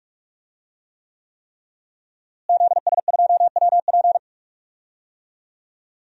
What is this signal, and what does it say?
Morse code, heard as a keyed tone, about 41 wpm, 700 Hz. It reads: ZS2WP
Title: DXCC-100 Call Signs - 5 Characters - Repeated 3x - 45wpm